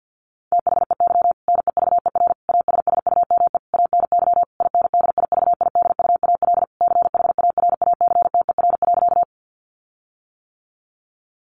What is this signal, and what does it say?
Morse code, heard as a keyed tone, about 48 wpm, 700 Hz. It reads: THEY NEVER ASSUME ANY INDIVIDUAL CHARACTER.
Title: Sentences from Top 1500 Words 50wpm